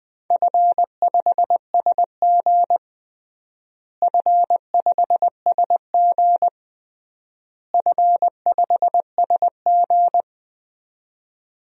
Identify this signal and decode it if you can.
Morse code, heard as a keyed tone, about 20 wpm, 700 Hz. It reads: F5SG F5SG F5SG
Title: DXCC-100 Call Signs - 4 Characters - Repeated 3x - 20wpm